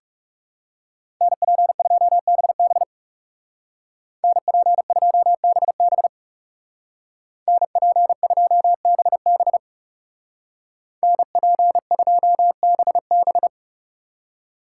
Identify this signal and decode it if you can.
Morse code, heard as a keyed tone, about 39 wpm, 700 Hz. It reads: NP2BB NP2BB NP2BB NP2BB